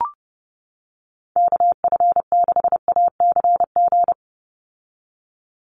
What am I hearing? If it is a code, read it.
Morse code, heard as a keyed tone, about 30 wpm, 700 Hz. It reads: KF6ACG
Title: USA Call Signs 30wpm